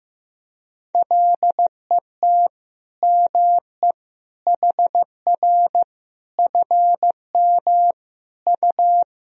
Morse code, heard as a keyed tone, about 15 wpm, 700 Hz. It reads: LET ME HR FM U
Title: Sentences from Top 400 Words and Abbreviations 15wpm